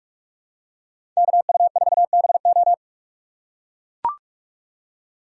Morse code, heard as a keyed tone, about 44 wpm, 700 Hz. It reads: KU4BO
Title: USA Call Signs - 5 Characters - 6x Character Spacing - 45wpm